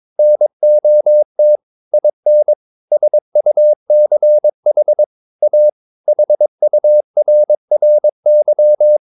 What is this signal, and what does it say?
Morse code, heard as a keyed tone, about 22 wpm, 600 Hz. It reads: NOT IN SUCH A HURRY
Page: Romanen Never af Ken Follet på engelsk.